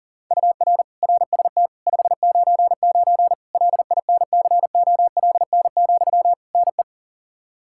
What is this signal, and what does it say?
Morse code, heard as a keyed tone, about 40 wpm, 700 Hz. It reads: UR RST 599 LINCOLN, NE
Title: Practice QSO - Exchange 1 - Beginning - 40wpm